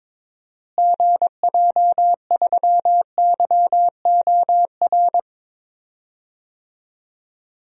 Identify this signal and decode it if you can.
Morse code, heard as a keyed tone, about 22 wpm, 700 Hz. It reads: GJ3YOR
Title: DXCC-100 Call Signs - 6 Characters - Repeated 3x - 22wpm